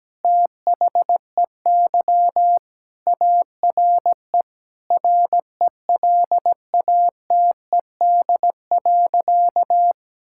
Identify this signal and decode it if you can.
Morse code, heard as a keyed tone, about 17 wpm, 700 Hz. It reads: THEY ARE RELATED.
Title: Sentences from Top 1000 Words - Review Part 5 - 17wpm